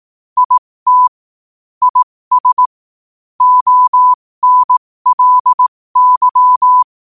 Morse code, heard as a keyed tone, about 18 wpm, spaced as for 15 wpm, 1000 Hz. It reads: IT IS ONLY
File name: qotdfiles / 15wpm